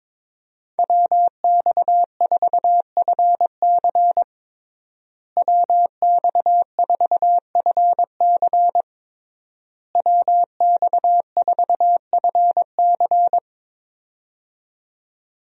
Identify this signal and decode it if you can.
Morse code, heard as a keyed tone, about 22 wpm, 700 Hz. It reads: WX4FC WX4FC WX4FC
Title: USA Call Signs - 5 Characters - Repeated 3x - 22wpm